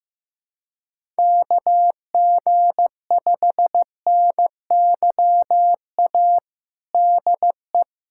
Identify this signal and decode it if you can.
Morse code, heard as a keyed tone, about 15 wpm, 700 Hz. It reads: KG5NYA DE
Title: USA Callsign DE Callsign - 6 Characters - 15wpm